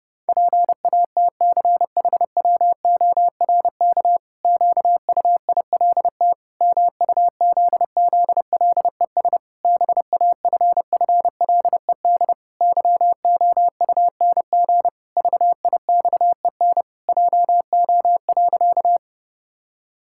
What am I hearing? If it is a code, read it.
Morse code, heard as a keyed tone, about 30 wpm, 700 Hz. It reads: PATCHWORK QUILT MUZZLES BAFFLED YOUNG VIXEN JO.
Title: Pangram Challenge v2 - 30wpm